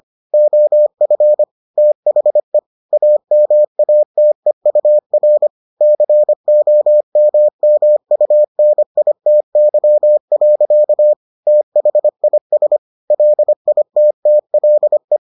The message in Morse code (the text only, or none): OF THE AMATEUR COMMUNITY. THIS LITTLE